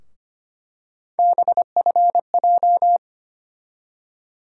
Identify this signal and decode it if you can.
Morse code, heard as a keyed tone, about 25 wpm, 700 Hz. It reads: BFJ
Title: USA Call Sign Postfixes - 3 Letters 25wpm